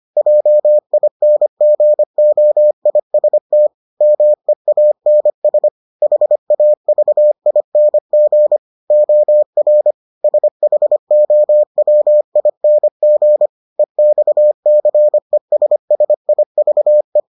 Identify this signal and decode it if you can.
Morse code, heard as a keyed tone, about 25 wpm, 600 Hz. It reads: JINGOIST MEANS HAVING OR SHOWING EXCESSIVE